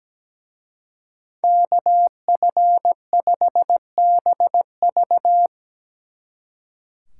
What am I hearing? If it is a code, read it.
Morse code, heard as a keyed tone, about 19 wpm, 700 Hz. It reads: KF5BV
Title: Speed Racing - USA 2x2 Call Signs 17wpm